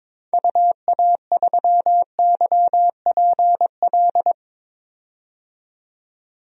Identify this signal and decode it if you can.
Morse code, heard as a keyed tone, about 22 wpm, 700 Hz. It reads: UA3YPL
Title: DXCC-100 Call Signs - 6 Characters - 22wpm